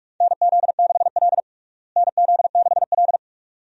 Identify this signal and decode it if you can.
Morse code, heard as a keyed tone, about 46 wpm, 700 Hz. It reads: NZ6L NZ6L
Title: USA Call Signs - 4 Characters - Repeated 2x - 45wpm